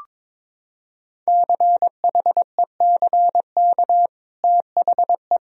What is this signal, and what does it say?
Morse code, heard as a keyed tone, about 22 wpm, 700 Hz. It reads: CHECK THE
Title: Sentences from Top 1000 Words - Review Part 3 - 22wpm